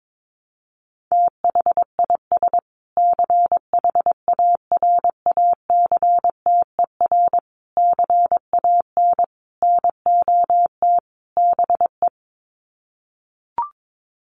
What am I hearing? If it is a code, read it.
Morse code, heard as a keyed tone, about 22 wpm, 700 Hz. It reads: THIS CHARACTER CAN NOT BE
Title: Sets of 5 Words Encore2 22wpm